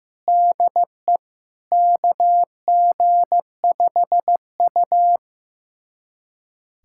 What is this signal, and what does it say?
Morse code, heard as a keyed tone, about 15 wpm, 700 Hz. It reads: DE KG5U
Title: USA Callsign DE Callsign - 4 Characters - 15wpm